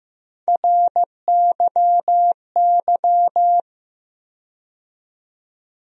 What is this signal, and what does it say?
Morse code, heard as a keyed tone, about 15 wpm, 700 Hz. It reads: RYY
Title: USA Call Sign Postfixes - 3 Letters 15wpm